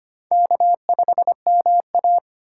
Morse code, heard as a keyed tone, about 25 wpm, 700 Hz. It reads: K5MA
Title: K1USN SST QSO Practice - 25wpm